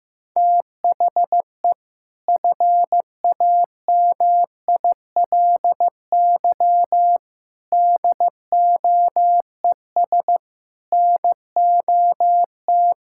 Morse code, heard as a keyed tone, about 15 wpm, 700 Hz. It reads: THE FAMILY DOES NOT
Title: Sentences from Top 1600 Words 15wpm